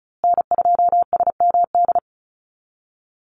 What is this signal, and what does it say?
Morse code, heard as a keyed tone, about 34 wpm, 700 Hz. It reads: N2SMD
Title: USA Call Signs Repeated 2x 35wpm